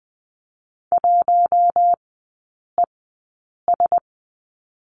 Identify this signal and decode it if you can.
Morse code, heard as a keyed tone, about 20 wpm, spaced as for 8 wpm, 700 Hz. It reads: 1ES
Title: 047 Lesson 9e - Number 4 - Three Characters 20wpm